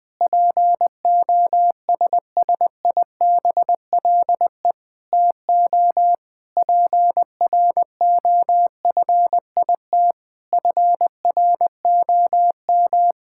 Morse code, heard as a keyed tone, about 21 wpm, 700 Hz. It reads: POSSIBLE TO PROFIT FROM
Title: Sentences from Top 1000 Words - Review Part 5 - 20wpm